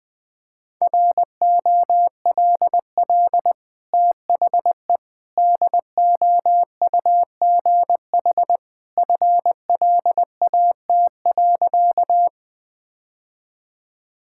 Morse code, heard as a keyed tone, about 20 wpm, 700 Hz. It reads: ROLL THE DOUGH FLAT.
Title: Easy Sentences with Hints - 20wpm